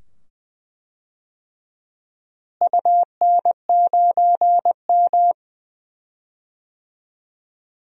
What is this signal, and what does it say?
Morse code, heard as a keyed tone, about 20 wpm, 700 Hz. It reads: UN9M
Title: DXCC-100 Call Signs - 4 Characters - Repeated 2x - 20wpm